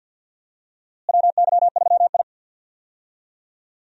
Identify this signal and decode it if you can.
Morse code, heard as a keyed tone, about 52 wpm, 700 Hz. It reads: WY3I
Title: USA Call Signs - 4 Characters - Repeated 3x - 50wpm